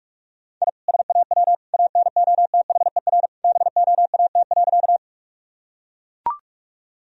Morse code, heard as a keyed tone, about 41 wpm, 700 Hz. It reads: I SAW ANOTHER BOAT.
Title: Sentences from Top 1000 Words - Review Part 3 - 45wpm